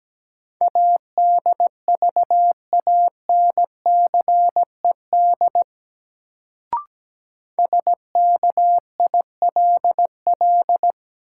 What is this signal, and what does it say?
Morse code, heard as a keyed tone, about 17 wpm, 700 Hz. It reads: ADVANCED SKILL
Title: Sentences from Top 1200 Words - New Word Drill Alt - 17wpm